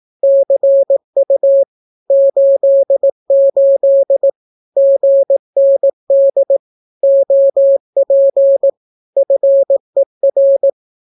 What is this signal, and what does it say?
Morse code, heard as a keyed tone, about 18 wpm, 550 Hz. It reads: CU 88 GND OP FER
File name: naukapodstaw / skroty / skroty